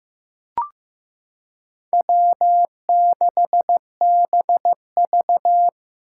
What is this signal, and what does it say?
Morse code, heard as a keyed tone, about 15 wpm, 700 Hz. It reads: W6BV
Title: Four Challenging Letter-Number Combinations 15wpm